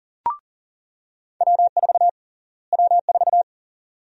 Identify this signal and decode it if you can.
Morse code, heard as a keyed tone, about 40 wpm, 700 Hz. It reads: W4 W4
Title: USA Call Sign Prefixes - Repeated 2x - 40wpm